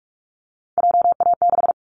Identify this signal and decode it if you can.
Morse code, heard as a keyed tone, about 44 wpm, 700 Hz. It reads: JA6
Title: Callsign Prefixes for Top 50 DXCC - 45wpm